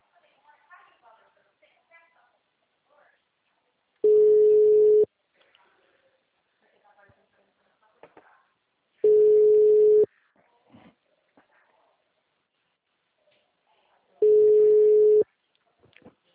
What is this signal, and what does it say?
Ringback tone (Europe, 425 Hz), with speech.
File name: out- -2002- - - .1942